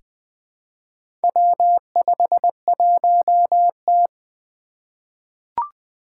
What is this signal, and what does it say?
Morse code, heard as a keyed tone, about 20 wpm, 700 Hz. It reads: W51T
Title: Four Challenging Letter-Number Combinations 20wpm